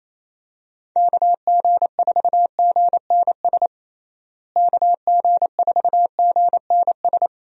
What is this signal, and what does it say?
Morse code, heard as a keyed tone, about 27 wpm, 700 Hz. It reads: KG4GNS KG4GNS
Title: USA Call Signs - 6 Characters - Repeated 2x - 28wpm